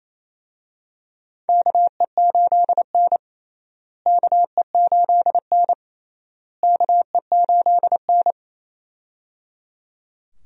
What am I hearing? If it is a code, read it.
Morse code, heard as a keyed tone, about 28 wpm, 700 Hz. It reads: KE8N KE8N KE8N
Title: USA Call Signs - 4 Characters - Repeated 3x - 28wpm